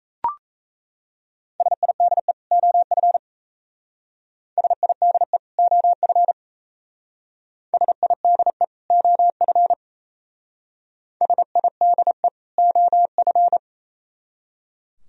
Morse code, heard as a keyed tone, about 36 wpm, 700 Hz. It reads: SIDE OF SIDE OF SIDE OF SIDE OF